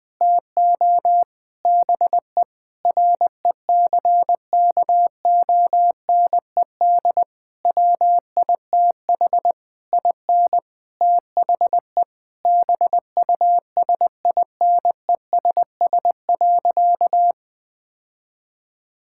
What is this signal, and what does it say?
Morse code, heard as a keyed tone, about 21 wpm, 700 Hz. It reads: TO BE RECKONED WITH IN THE BUSINESS.
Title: Sentences with 8-Word Phrases - Spoken Beginning - 20wpm